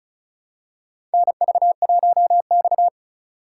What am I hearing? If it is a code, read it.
Morse code, heard as a keyed tone, about 36 wpm, 700 Hz. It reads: NV1X
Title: USA Call Signs - 4 Characters - 3x Character Spacing - 35wpm